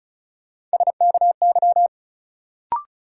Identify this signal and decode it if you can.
Morse code, heard as a keyed tone, about 36 wpm, 700 Hz. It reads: SKY
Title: Sentences from Top 1200 Words - New Word Drill Alt - 35wpm